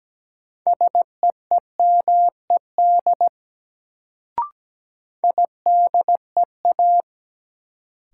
Morse code, heard as a keyed tone, about 17 wpm, 700 Hz. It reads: SEEMED IDEA
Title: Sets of 3 Words - New Word Drill - 17wpm